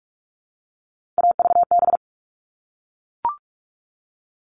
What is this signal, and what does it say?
Morse code, heard as a keyed tone, about 44 wpm, 700 Hz. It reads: AVB
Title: All Cut Numbers - 3-Digit 45wpm